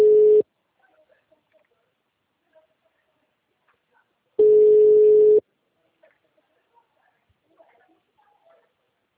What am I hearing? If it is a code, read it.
Ringback tone (Europe, 425 Hz).
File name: out- -2003- - - .53150